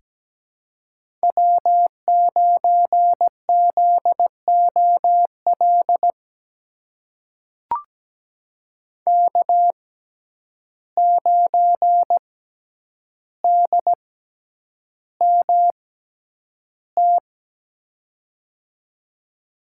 Morse code, heard as a keyed tone, about 17 wpm, spaced as for 9 wpm, 700 Hz. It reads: W9ZOL K9DMT
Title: USA Call Signs - 5 Characters - 6x Character Spacing - 17wpm